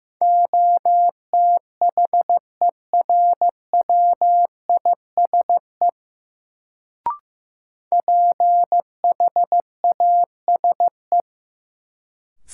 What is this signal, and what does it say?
Morse code, heard as a keyed tone, about 16 wpm, 700 Hz. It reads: OTHERWISE PHASE
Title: Sentences from Top 1200 Words - New Word Drill Alt - 15wpm